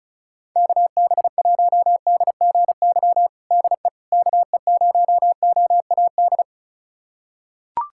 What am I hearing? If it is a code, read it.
Morse code, heard as a keyed tone, about 34 wpm, 700 Hz. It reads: KB1DGY DE KE0OAD
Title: USA Callsign DE Callsign - 6 Characters - 35wpm